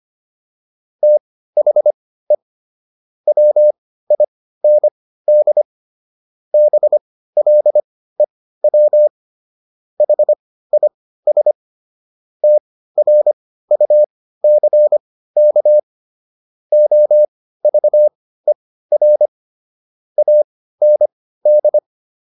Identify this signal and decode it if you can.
Morse code, heard as a keyed tone, about 25 wpm, spaced as for 15 wpm, 600 Hz. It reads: THE WIND BLEW HIS TRUCK OVER AND